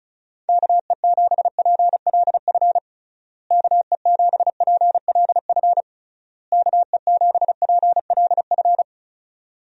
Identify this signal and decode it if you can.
Morse code, heard as a keyed tone, about 34 wpm, 700 Hz. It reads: KE7PLF KE7PLF KE7PLF
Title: USA Call Signs - 6 Characters - Repeated 3x - 35wpm